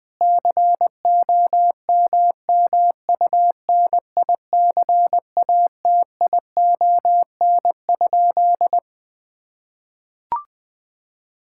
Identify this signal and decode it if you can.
Morse code, heard as a keyed tone, about 20 wpm, 700 Hz. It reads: COMMUNICATION?
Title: Sentences from Top 1000 Words - Review Part 6 - 20wpm